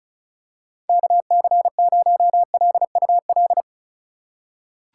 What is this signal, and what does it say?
Morse code, heard as a keyed tone, about 34 wpm, 690 Hz. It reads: KC0LUL
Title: USA Call Signs - 6 Characters - 6x Character Spacing - 35wpm